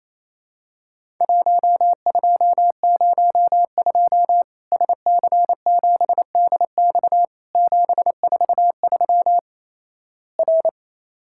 Morse code, heard as a keyed tone, about 27 wpm, 700 Hz. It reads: 1202 SC7DX 743 R
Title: Worked All Europe DX Contest - QTC Exchange - 28wpm